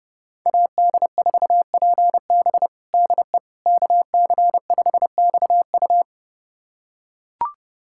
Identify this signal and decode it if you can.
Morse code, heard as a keyed tone, about 30 wpm, 700 Hz. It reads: AD4PB DE KC5XU
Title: USA Callsign DE Callsign - 5 Characters - 30wpm